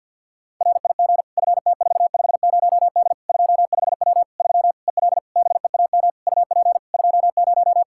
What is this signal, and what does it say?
Morse code, heard as a keyed tone, about 49 wpm, 700 Hz. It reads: RIG FT450D 25W 3 EL BEAM UP 20
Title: Practice QSO - Exchange 2 - Info - 50wpm